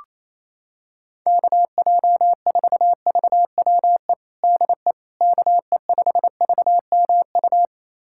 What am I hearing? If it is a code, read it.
Morse code, heard as a keyed tone, about 28 wpm, 700 Hz. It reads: KJ4VWE DE KE5VMU
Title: USA Callsign DE Callsign - 6 Characters - 28wpm